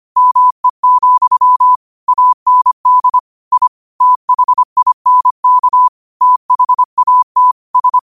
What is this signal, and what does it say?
Morse code, heard as a keyed tone, about 25 wpm, 1000 Hz. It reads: ME, AND I THINK THAT S
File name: qotdfiles / 25wpm